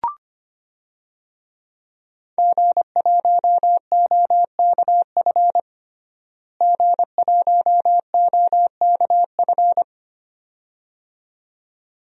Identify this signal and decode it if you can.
Morse code, heard as a keyed tone, about 25 wpm, 700 Hz. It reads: G1OKF G1OKF
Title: DXCC-100 Call Signs - 5 Characters - Repeated 2x - 25wpm